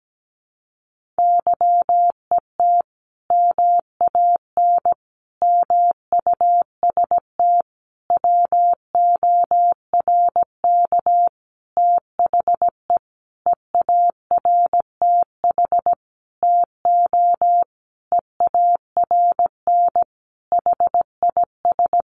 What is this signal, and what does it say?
Morse code, heard as a keyed tone, about 17 wpm, 700 Hz. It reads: YET MAN MUST WORK THE EARTH TO EARN HIS
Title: Sentences from Top 1500 Words 17wpm